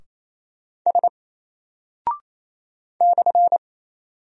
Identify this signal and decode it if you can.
Morse code, heard as a keyed tone, about 27 wpm, 700 Hz. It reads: S /
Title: Single Character (Letter:Number:Punctuation:Prosign) 28wpm